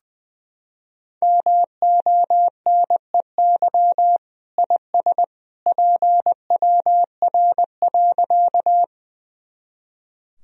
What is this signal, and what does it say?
Morse code, heard as a keyed tone, about 20 wpm, 700 Hz. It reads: MONEY IS PWR.
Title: Sentences from Top 500 Words and Abbreviations 20wpm